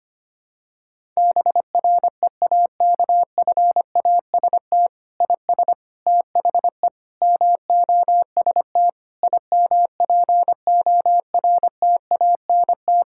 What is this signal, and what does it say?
Morse code, heard as a keyed tone, about 25 wpm, 700 Hz. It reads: BREAKFAST IS THE MOST IMPORTANT
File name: practice / 25wpm / Difficult Sentences with Hints - 25wpm